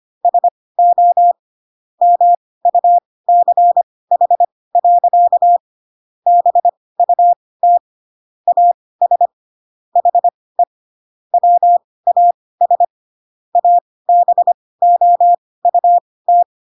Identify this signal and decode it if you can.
Morse code, heard as a keyed tone, about 25 wpm, spaced as for 18 wpm, 700 Hz. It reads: SO MUCH. BUT AS HE WAS ABOUT
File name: Pinocchio / Pinocchio ch2 12 25 ik1hge